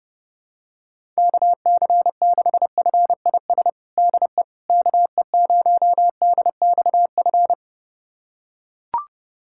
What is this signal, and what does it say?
Morse code, heard as a keyed tone, about 30 wpm, 700 Hz. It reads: KC6FIS DE KE0DXF
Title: USA Callsign DE Callsign - 6 Characters - 30wpm